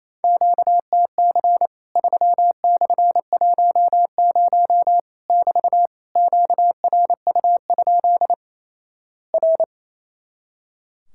Morse code, heard as a keyed tone, about 28 wpm, 700 Hz. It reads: QTC 3/10 <BT> QRU? R
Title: Worked All Europe DX Contest - QTC Exchange - 28wpm